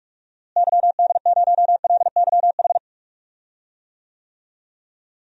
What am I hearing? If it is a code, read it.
Morse code, heard as a keyed tone, about 42 wpm, 700 Hz. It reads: YD0LYH
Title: DXCC-100 Call Signs - 6 Characters - Repeated 2x - 45wpm